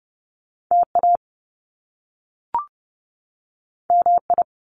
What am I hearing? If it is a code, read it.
Morse code, heard as a keyed tone, about 30 wpm, 700 Hz. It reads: TA MI
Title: Top Two Letter Combinations in Words 30wpm